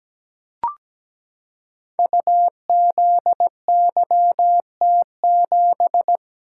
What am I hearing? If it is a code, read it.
Morse code, heard as a keyed tone, about 17 wpm, 700 Hz. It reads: UZYT7
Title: Code Groups - 5 Letter:Numbers - 17wpm